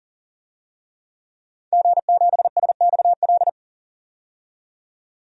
Morse code, heard as a keyed tone, about 40 wpm, 700 Hz. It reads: G7SXL
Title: DXCC-100 Call Signs - 5 Characters - 40wpm